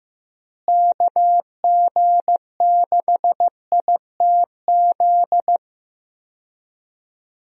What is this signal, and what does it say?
Morse code, heard as a keyed tone, about 15 wpm, 700 Hz. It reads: KG6ITZ
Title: USA Call Signs - 6 Characters - Repeated 2x - 15wpm